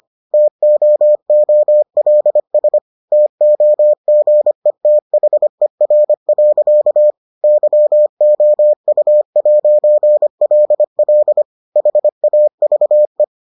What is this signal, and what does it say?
Morse code, heard as a keyed tone, about 25 wpm, 600 Hz. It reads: TOOLS TOGETHER. YOU'LL HAVE